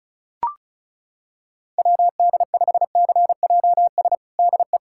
Morse code, heard as a keyed tone, about 36 wpm, 700 Hz. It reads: WD5CJS DE
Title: USA Callsign DE Callsign 35wpm